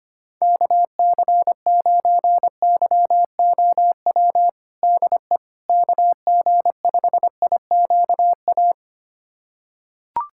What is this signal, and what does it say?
Morse code, heard as a keyed tone, about 25 wpm, 700 Hz. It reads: KC9YOW DE KG5IQA
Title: USA Callsign DE Callsign - 6 Characters - 25wpm